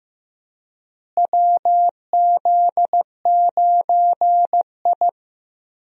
Morse code, heard as a keyed tone, about 16 wpm, 700 Hz. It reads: WZ9I
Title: USA Call Signs - 4 Characters - Repeated 3x - 15wpm